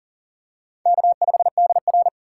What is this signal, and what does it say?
Morse code, heard as a keyed tone, about 40 wpm, 700 Hz. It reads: K5DR
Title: USA Call Signs - 8x Character Spacing - 40wpm